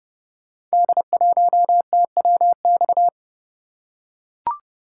Morse code, heard as a keyed tone, about 28 wpm, 700 Hz. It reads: D1TWX
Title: Code Groups - 5 Letter:Numbers - 30wpm